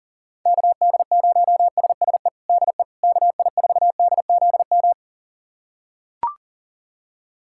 Morse code, heard as a keyed tone, about 40 wpm, 700 Hz. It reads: KD0SSE DE KI4DZM
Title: USA Callsign DE Callsign 40wpm